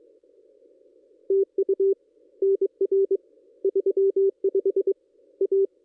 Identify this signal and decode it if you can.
Morse code, heard as a keyed tone, about 22 wpm, 390 Hz. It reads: TU NR 35 A
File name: contests / ss-cw-k5tr-op-n5rz -1800